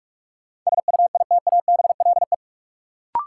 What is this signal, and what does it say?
Morse code, heard as a keyed tone, about 47 wpm, 700 Hz. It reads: SUITABLE
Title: Sentences from Top 1400 Words - New Word Drill - 45wpm